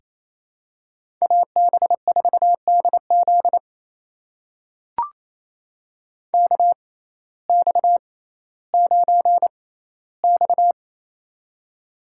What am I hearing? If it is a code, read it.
Morse code, heard as a keyed tone, about 28 wpm, spaced as for 15 wpm, 700 Hz. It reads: AB4DZ KX9X
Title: USA Call Signs - 6x Character Spacing - 28wpm